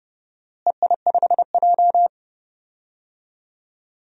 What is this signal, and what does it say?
Morse code, heard as a keyed tone, about 30 wpm, 700 Hz. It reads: EI5J
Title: DXCC-100 Call Signs - 4 Characters - Repeated 2x - 30wpm